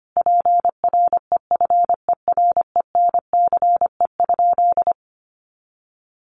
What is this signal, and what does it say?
Morse code, heard as a keyed tone, about 25 wpm, 700 Hz. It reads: PREFERENCE?
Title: Sentences from Top 1500 Words 25wpm